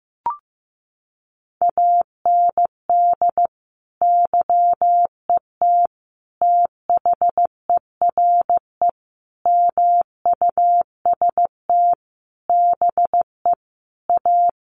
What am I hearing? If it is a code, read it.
Morse code, heard as a keyed tone, about 15 wpm, 700 Hz. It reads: AND YET THERE MUST BE A
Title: Sentences from Top 1300 Words 15wpm